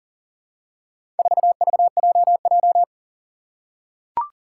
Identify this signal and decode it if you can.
Morse code, heard as a keyed tone, about 40 wpm, 700 Hz. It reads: 4VJJ
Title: Four Challenging Letter-Number Combinations 40wpm